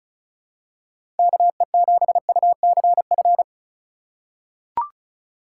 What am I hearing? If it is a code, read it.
Morse code, heard as a keyed tone, about 36 wpm, 700 Hz. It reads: KE7UCF
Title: USA Call Signs - 4x Character Spacing - 35wpm